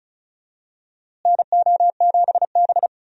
Morse code, heard as a keyed tone, about 34 wpm, 700 Hz. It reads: NO7B
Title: USA Call Signs - 4 Characters - Repeated 2x - 35wpm